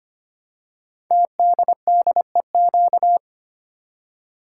Morse code, heard as a keyed tone, about 25 wpm, 700 Hz. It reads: TDDEQ
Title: Code Groups - 5 Letters - 25wpm